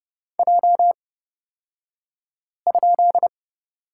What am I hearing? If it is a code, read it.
Morse code, heard as a keyed tone, about 30 wpm, spaced as for 5 wpm, 700 Hz. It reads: J?
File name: Lesson 37d - Letter x - Two Characters 30wpm